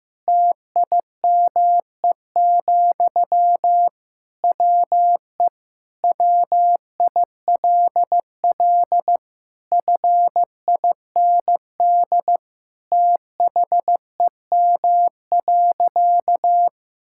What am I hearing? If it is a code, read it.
Morse code, heard as a keyed tone, about 15 wpm, 700 Hz. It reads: TIME, WE WILL FIND THEM.
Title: Sentences from Top 1000 Words - Review Part 5 - 15wpm